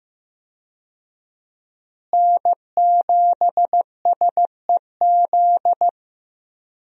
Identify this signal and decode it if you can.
Morse code, heard as a keyed tone, about 15 wpm, 700 Hz. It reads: N7SEZ
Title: USA Call Signs - 5 Characters - Repeated 2x - 15wpm